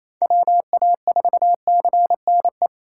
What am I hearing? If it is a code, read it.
Morse code, heard as a keyed tone, about 28 wpm, 700 Hz. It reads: WA4CNE
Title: USA Call Signs - 6 Characters - 3x Character Spacing - 28wpm